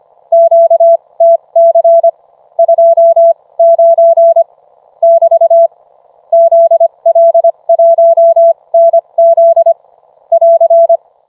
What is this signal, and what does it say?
Morse code, heard as a keyed tone, about 25 wpm, spaced as for 21 wpm, 670 Hz. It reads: QTC 29 <BT> ZL1NZ <AR>